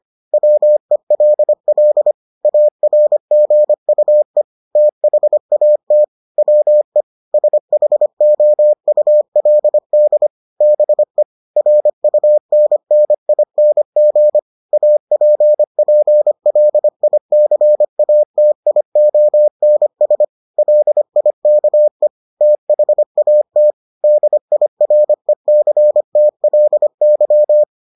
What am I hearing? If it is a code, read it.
Morse code, heard as a keyed tone, about 25 wpm, 600 Hz. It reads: WELL ARGUE THAT WE SHOULD BE RUNNING APPLICATIONS LIKE THAT DIRECTLY